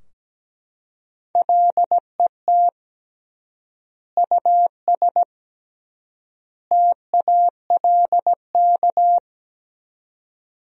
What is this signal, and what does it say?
Morse code, heard as a keyed tone, about 17 wpm, 700 Hz. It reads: LET US TALK
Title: 3x Word-Spacing - Sentences from Top 400 Words - 17wpm